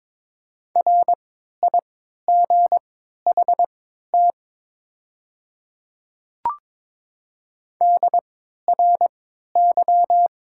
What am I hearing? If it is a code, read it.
Morse code, heard as a keyed tone, about 22 wpm, spaced as for 12 wpm, 700 Hz. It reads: RIGHT DRY
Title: 3x Character Spacing - Top 500 Words - 22wpm